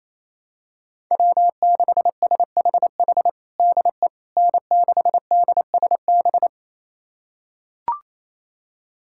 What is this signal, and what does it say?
Morse code, heard as a keyed tone, about 28 wpm, 700 Hz. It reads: W6SHH DE N6DSB
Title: USA Callsign DE Callsign - 5 Characters - 28wpm